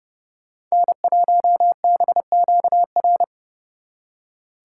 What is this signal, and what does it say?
Morse code, heard as a keyed tone, about 36 wpm, 700 Hz. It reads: N1BQR
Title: Speed Racing - USA Call Signs 30wpm